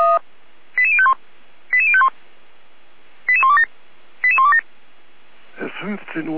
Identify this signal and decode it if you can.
Five-tone selective call (ZVEI), heard as quick runs of five tones, tones of about 70 ms, with speech. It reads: 88051 88051 88127 88127